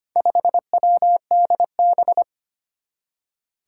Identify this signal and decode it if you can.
Morse code, heard as a keyed tone, about 25 wpm, 700 Hz. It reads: 5WDB
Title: Four Challenging Letter-Number Combinations 25wpm